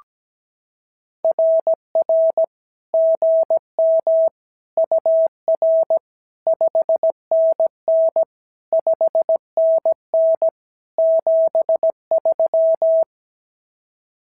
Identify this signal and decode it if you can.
Morse code, heard as a keyed tone, about 17 wpm, 650 Hz. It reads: RR GM UR 5NN 5NN 73
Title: SOTA QSO Practice 17wpm